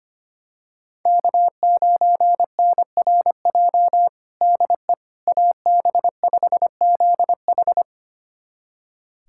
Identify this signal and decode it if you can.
Morse code, heard as a keyed tone, about 25 wpm, 700 Hz. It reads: K9NRJ DE AB5ZH
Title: USA Callsign DE Callsign - 5 Characters - 25wpm